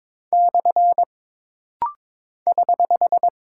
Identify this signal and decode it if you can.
Morse code, heard as a keyed tone, about 22 wpm, 700 Hz. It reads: / <HH>
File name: practice / 22wpm / Single Punctuation-Prosign - Rapid-Fire - 22wpm